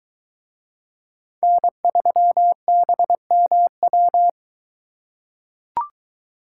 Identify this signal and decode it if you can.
Morse code, heard as a keyed tone, about 21 wpm, 700 Hz. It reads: N3BMW